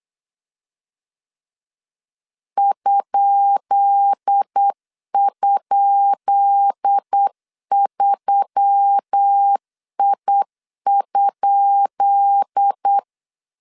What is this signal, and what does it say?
Morse code, heard as a keyed tone, about 8 wpm, 800 Hz. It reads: ??3I?
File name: LEZIONI / 10 SIGNS / 10B INT 8wpm 3534